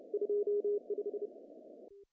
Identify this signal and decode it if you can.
Morse code, heard as a keyed tone, about 32 wpm, 410 Hz. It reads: 25
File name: CW / cqww-cw-k5tr